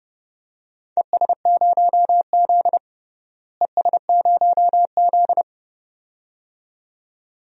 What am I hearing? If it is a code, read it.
Morse code, heard as a keyed tone, about 30 wpm, 700 Hz. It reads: ES0Z ES0Z
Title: DXCC-100 Call Signs - 4 Characters - Repeated 2x - 30wpm